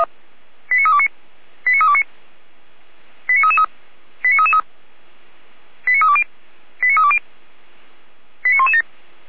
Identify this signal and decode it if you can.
Five-tone selective call (ZVEI), heard as quick runs of five tones, tones of about 70 ms. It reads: 89329 89329 89333 89333 89320 89320 89117
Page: ZVEI